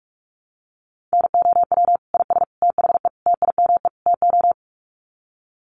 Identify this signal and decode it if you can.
Morse code, heard as a keyed tone, about 44 wpm, 700 Hz. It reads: NOW IS THE TIME TO
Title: Sets of 5 Words Encore2 45wpm